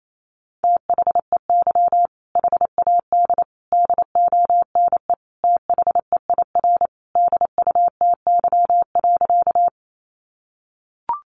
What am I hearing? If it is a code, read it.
Morse code, heard as a keyed tone, about 28 wpm, 700 Hz. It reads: THEY HAD DONE THEIR DUTY.
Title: Sentences from Top 1500 Words 28wpm